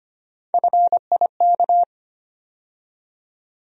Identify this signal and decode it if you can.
Morse code, heard as a keyed tone, about 25 wpm, 700 Hz. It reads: FIK
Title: USA Call Sign Postfixes - 3 Letters 25wpm